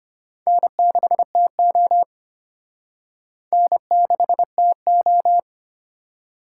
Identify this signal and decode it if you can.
Morse code, heard as a keyed tone, about 31 wpm, 700 Hz. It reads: N6TO N6TO
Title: Speed Racing - USA 1x2 Call Signs 25wpm